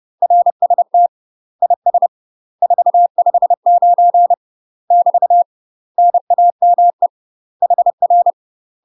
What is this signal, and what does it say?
Morse code, heard as a keyed tone, about 30 wpm, 700 Hz. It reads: RST IS 459 <BT> NAME HR
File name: UK MASTER QSOs TRAINING / 30WPM / UK MASTER QSOs 0045